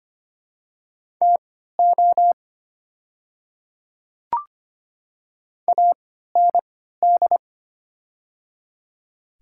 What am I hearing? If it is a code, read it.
Morse code, heard as a keyed tone, about 25 wpm, spaced as for 14 wpm, 690 Hz. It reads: TO AND
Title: 3x Character Spacing - Top 200 Words - 25wpm